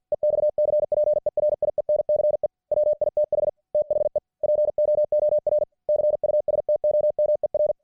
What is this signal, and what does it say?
Morse code, heard as a keyed tone, about 53 wpm, 600 Hz. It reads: EXCPERIENCE WITH THE POOR CUSTOMER